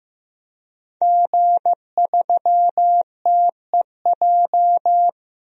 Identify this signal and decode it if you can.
Morse code, heard as a keyed tone, about 16 wpm, 700 Hz. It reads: G3TEJ
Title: DXCC-100 Call Signs - 5 Characters - Repeated 3x - 15wpm